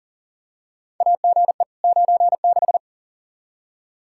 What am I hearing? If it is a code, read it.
Morse code, heard as a keyed tone, about 40 wpm, 700 Hz. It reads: AGE 96
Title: Practice QSO - Exchange 3 - More Info - 40wpm